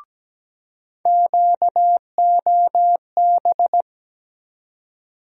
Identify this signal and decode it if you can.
Morse code, heard as a keyed tone, about 17 wpm, 700 Hz. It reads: QOB
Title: USA Call Sign Postfixes - 3 Letters 17wpm